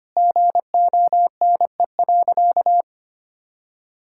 Morse code, heard as a keyed tone, about 25 wpm, 700 Hz. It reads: GONE.
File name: practice / 25wpm / Sentences with 4-Word Phrases - 25wpm